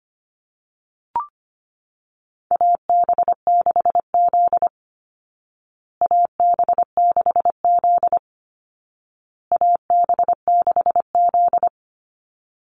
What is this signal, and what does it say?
Morse code, heard as a keyed tone, about 25 wpm, 700 Hz. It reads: AB6Z AB6Z AB6Z
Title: Call Signs Repeated 2x - 25wpm